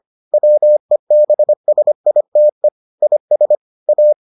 Morse code, heard as a keyed tone, about 25 wpm, 600 Hz. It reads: WEBSITE IS A